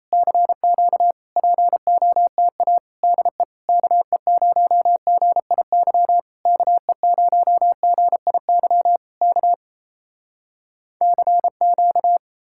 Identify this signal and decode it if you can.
Morse code, heard as a keyed tone, about 33 wpm, 700 Hz. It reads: CQ POTA DE KE0GIY KE0GIY K CQ
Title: Speed-Racing Common QSO Phrases Encore - 28wpm